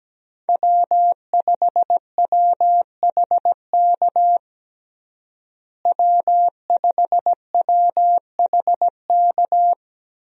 Morse code, heard as a keyed tone, about 17 wpm, 700 Hz. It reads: W5WHK W5WHK
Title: USA Call Signs - 5 Characters - Repeated 3x - 17wpm